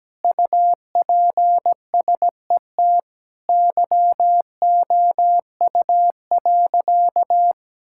Morse code, heard as a keyed tone, about 17 wpm, 700 Hz. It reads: UPSET YOU.
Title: Sentences from Top 1000 Words - Review Part 5 - 17wpm